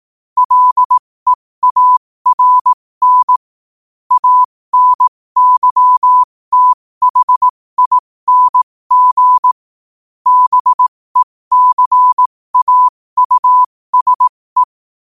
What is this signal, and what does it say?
Morse code, heard as a keyed tone, about 18 wpm, spaced as for 15 wpm, 1000 Hz. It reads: LEARN ANYTHING BECAUSE
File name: qotdfiles / 15wpm